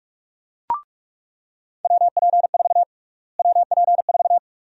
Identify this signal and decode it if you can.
Morse code, heard as a keyed tone, about 46 wpm, 700 Hz. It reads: WP4 WP4
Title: USA Call Sign Prefixes - Repeated 2x - 45wpm